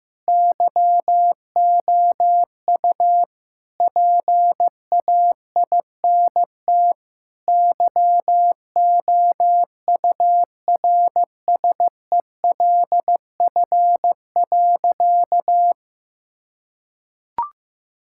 Morse code, heard as a keyed tone, about 15 wpm, 700 Hz. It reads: YOU PAINT YOURSELF.
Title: Sentences from Top 1000 Words - Review Part 6 - 15wpm